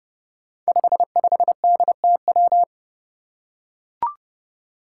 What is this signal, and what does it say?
Morse code, heard as a keyed tone, about 30 wpm, 700 Hz. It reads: 55DTW
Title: Five Challenging Letter-Number Combinations 30wpm